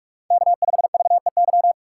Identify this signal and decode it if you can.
Morse code, heard as a keyed tone, about 47 wpm, 700 Hz. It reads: K5VEY
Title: USA Call Signs - 5 Characters - 2x Character Spacing - 45wpm